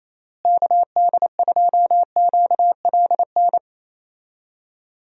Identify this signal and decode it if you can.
Morse code, heard as a keyed tone, about 28 wpm, 700 Hz. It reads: KD2QLN
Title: USA Call Signs - 6 Characters - Repeated 2x - 28wpm